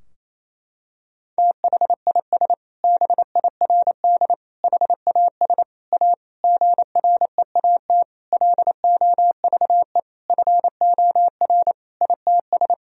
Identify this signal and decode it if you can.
Morse code, heard as a keyed tone, about 28 wpm, 700 Hz. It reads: THIS BIRD HAS A GREAT LOVE FOR ITS
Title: Sentences from Top 1000 Words - Review Part 2 - 28wpm